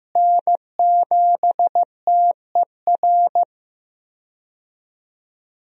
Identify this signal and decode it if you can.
Morse code, heard as a keyed tone, about 15 wpm, 700 Hz. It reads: N7TER
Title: USA Call Signs - 5 Characters - Repeated 3x - 15wpm